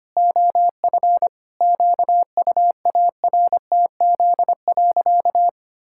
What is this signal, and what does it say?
Morse code, heard as a keyed tone, about 25 wpm, 700 Hz. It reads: OF QUARTZ.
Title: Pangram Challenge v2 - 25wpm